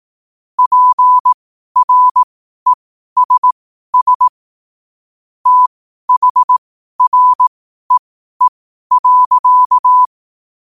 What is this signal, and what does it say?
Morse code, heard as a keyed tone, about 18 wpm, spaced as for 12 wpm, 1000 Hz. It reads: PRESS THREE.
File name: qotdfiles / 12wpm